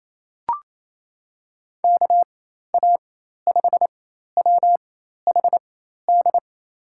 Morse code, heard as a keyed tone, about 28 wpm, spaced as for 13 wpm, 700 Hz. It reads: KA5WHD
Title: USA Call Signs - 6 Characters - 4x Character Spacing - 28wpm